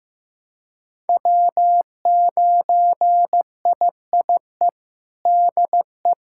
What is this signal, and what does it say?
Morse code, heard as a keyed tone, about 15 wpm, 700 Hz. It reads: W9IIE DE
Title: USA Callsign DE Callsign - 5 Characters - 15wpm